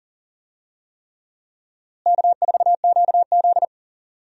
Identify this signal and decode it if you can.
Morse code, heard as a keyed tone, about 40 wpm, 700 Hz. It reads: K4QZ
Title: USA Call Signs - 4 Characters - Repeated 3x - 40wpm